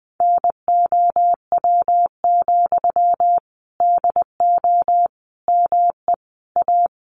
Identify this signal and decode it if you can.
Morse code, heard as a keyed tone, about 20 wpm, 700 Hz. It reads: NOW, DO ME A
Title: Sentences from Top 700 Words 20wpm